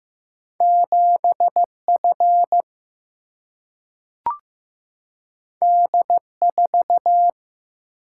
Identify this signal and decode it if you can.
Morse code, heard as a keyed tone, about 15 wpm, 700 Hz. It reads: 7 F D 4
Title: Single-Letter Single-Number 15wpm